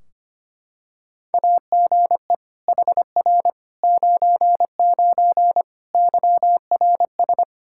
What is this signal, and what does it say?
Morse code, heard as a keyed tone, about 25 wpm, 700 Hz. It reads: AGE HR 99 YRS
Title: Practice QSO - Exchange 3 - More Info - 25wpm